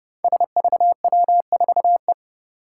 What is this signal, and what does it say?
Morse code, heard as a keyed tone, about 32 wpm, 700 Hz. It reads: SVW4E
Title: Five Challenging Letter-Number Combinations 30wpm